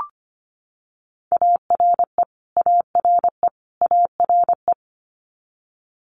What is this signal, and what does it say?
Morse code, heard as a keyed tone, about 25 wpm, 700 Hz. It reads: ARE ARE ARE
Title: Multi-letter Phonemes Repeated 3x 25wpm